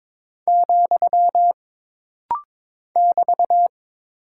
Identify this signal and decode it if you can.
Morse code, heard as a keyed tone, about 22 wpm, 700 Hz. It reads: , <BT>
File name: practice / 22wpm / Single Punctuation-Prosign - Rapid-Fire - 22wpm